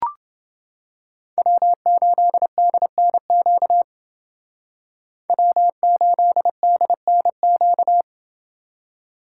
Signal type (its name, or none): Morse code